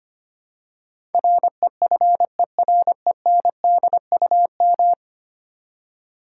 Morse code, heard as a keyed tone, about 25 wpm, 700 Hz. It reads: REFERENDUM
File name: Ten Letter Words - 25wpm